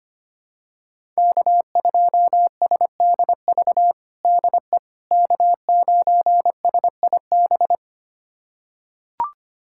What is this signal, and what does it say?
Morse code, heard as a keyed tone, about 25 wpm, 700 Hz. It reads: K2SDV DE K9SIB
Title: USA Callsign DE Callsign - 5 Characters - 25wpm